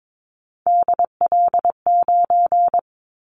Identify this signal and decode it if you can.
Morse code, heard as a keyed tone, about 22 wpm, 700 Hz. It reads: DL9
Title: Callsign Prefixes for Top 25 DXCC - Repeated 3x - 22wpm